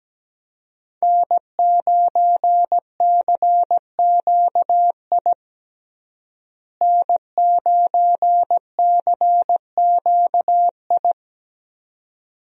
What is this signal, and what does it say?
Morse code, heard as a keyed tone, about 17 wpm, 700 Hz. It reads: N9CQI N9CQI
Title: USA Call Signs - 5 Characters - Repeated 2x - 17wpm